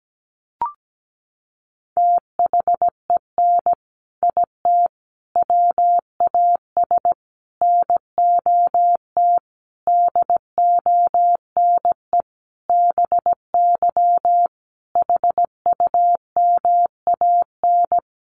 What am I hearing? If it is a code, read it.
Morse code, heard as a keyed tone, about 17 wpm, 700 Hz. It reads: THEN IT WAS NOT DONE BY HUMAN
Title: Sentences from Top 1300 Words 17wpm